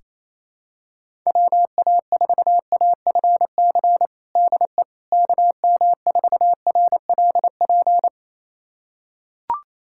Morse code, heard as a keyed tone, about 28 wpm, 700 Hz. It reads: WA4AFC DE KM4RLP
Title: USA Callsign DE Callsign - 6 Characters - 28wpm